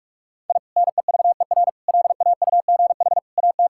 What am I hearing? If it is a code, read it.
Morse code, heard as a keyed tone, about 41 wpm, 700 Hz. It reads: I NEVER LAUGH AT
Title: Sentences from Top 1000 Words - Review Part 6 - 45wpm